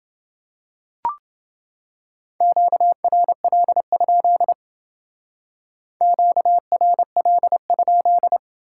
Morse code, heard as a keyed tone, about 30 wpm, 700 Hz. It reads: QRL? QRL?